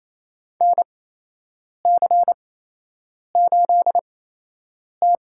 Morse code, heard as a keyed tone, about 28 wpm, spaced as for 8 wpm, 700 Hz. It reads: NC8T
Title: USA Call Signs - 4 Characters - 8x Character Spacing - 28wpm